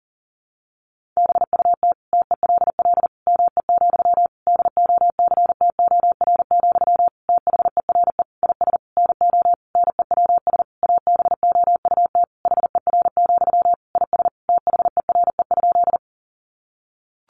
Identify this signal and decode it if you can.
Morse code, heard as a keyed tone, about 40 wpm, 700 Hz. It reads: BUT TELL ME, DOCTOR, THERE IS NO NEWS ABOUT HER, IS THERE?
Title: Sentences from Top 1500 Words 40wpm